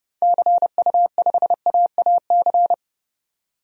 Morse code, heard as a keyed tone, about 30 wpm, 700 Hz. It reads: CU5AAC
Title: DXCC-100 Call Signs - 6 Characters - 30wpm